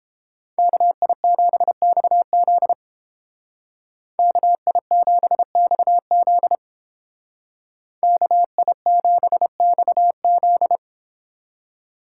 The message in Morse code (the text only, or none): KI7XZ KI7XZ KI7XZ